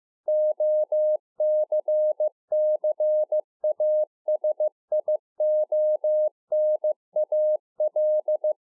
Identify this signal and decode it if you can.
Morse code, heard as a keyed tone, about 15 wpm, 610 Hz. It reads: OCCASIONAL
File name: 15WPM